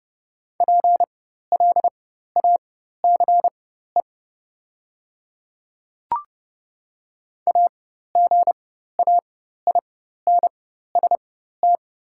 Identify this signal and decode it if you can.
Morse code, heard as a keyed tone, about 30 wpm, spaced as for 14 wpm, 700 Hz. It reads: PLACE AGAINST
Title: 4x Character Spacing - Top 500 Words - 30wpm